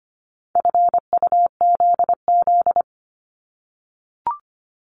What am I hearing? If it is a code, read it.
Morse code, heard as a keyed tone, about 25 wpm, 700 Hz. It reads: FUZZ
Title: Multi-letter Phonemes 25wpm